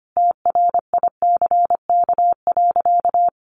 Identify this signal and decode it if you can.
Morse code, heard as a keyed tone, about 25 wpm, 700 Hz. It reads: TRICK.
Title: Sentences from Top 1500 Words 25wpm